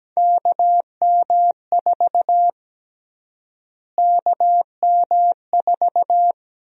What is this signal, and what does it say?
Morse code, heard as a keyed tone, about 17 wpm, 700 Hz. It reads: KM4 KM4
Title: USA Call Sign Prefixes - Repeated 2x - 17wpm